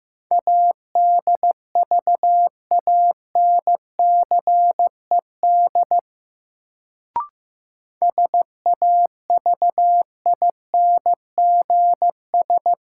Morse code, heard as a keyed tone, about 15 wpm, 700 Hz. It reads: ADVANCED SAVINGS
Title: Sentences from Top 1200 Words - New Word Drill - 15wpm